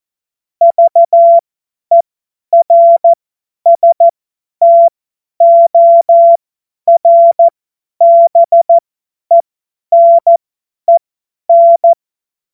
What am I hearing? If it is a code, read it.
Morse code, heard as a keyed tone, about 14 wpm, spaced as for 10 wpm, 680 Hz. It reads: VERSTORBENEN
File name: Morsekurs / Witze / Witz 17. Mai 2022 50 F70